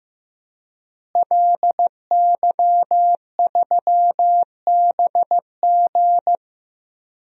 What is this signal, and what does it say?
Morse code, heard as a keyed tone, about 15 wpm, 700 Hz. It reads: LY3BG
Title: DXCC-100 Call Signs - 5 Characters - Repeated 2x - 15wpm